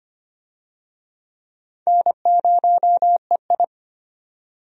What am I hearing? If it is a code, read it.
Morse code, heard as a keyed tone, about 25 wpm, 700 Hz. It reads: N0EI
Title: USA Call Signs - 4 Characters - 25wpm